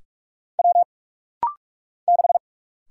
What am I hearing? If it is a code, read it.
Morse code, heard as a keyed tone, about 44 wpm, 700 Hz. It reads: W 6
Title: Single Letter-Number - Rapid-Fire 45wpm